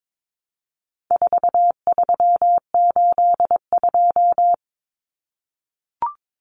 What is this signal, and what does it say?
Morse code, heard as a keyed tone, about 22 wpm, 700 Hz. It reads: 4382
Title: Common Cut Numbers - 4-Digit 22wpm